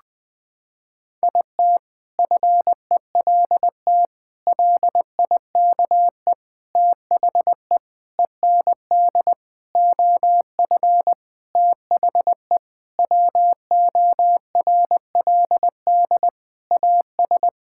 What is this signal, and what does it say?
Morse code, heard as a keyed tone, about 20 wpm, 700 Hz. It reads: IT FELT LIKE THE END OF THE WORLD AS
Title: Sentences with 8-Word Phrases - 20wpm